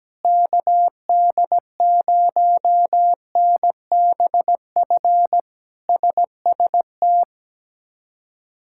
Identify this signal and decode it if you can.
Morse code, heard as a keyed tone, about 17 wpm, 700 Hz. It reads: KD0NBF SST
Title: K1USN SST QSO Practice - 17wpm